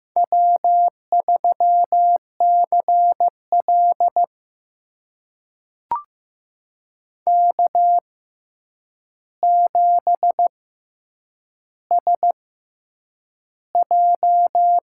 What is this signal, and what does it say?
Morse code, heard as a keyed tone, about 15 wpm, spaced as for 8 wpm, 700 Hz. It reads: W3CL K7SJ
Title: USA Call Signs - 4 Characters - 6x Character Spacing - 15wpm